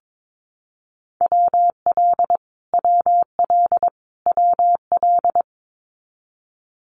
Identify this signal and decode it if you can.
Morse code, heard as a keyed tone, about 22 wpm, 700 Hz. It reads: WL WL WL
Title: Abbreviations Repeated 3x - 22wpm